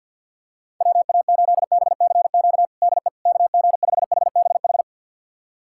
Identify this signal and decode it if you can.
Morse code, heard as a keyed tone, about 48 wpm, 690 Hz. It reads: WA8BCQ DE KG5HBH
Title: USA Callsign DE Callsign - 6 Characters - 50wpm